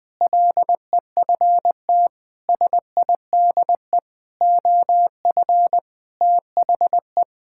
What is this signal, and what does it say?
Morse code, heard as a keyed tone, about 20 wpm, 700 Hz. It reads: LEFT SIDE OF THE
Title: Sentences with 6-Word Phrases - 20wpm